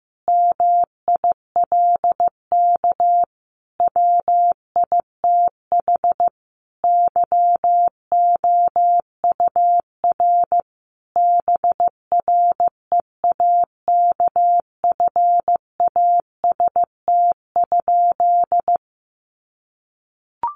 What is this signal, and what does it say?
Morse code, heard as a keyed tone, about 15 wpm, 700 Hz. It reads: MILK WITH YOUR BREAKFAST?
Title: Sentences from Top 1500 Words 15wpm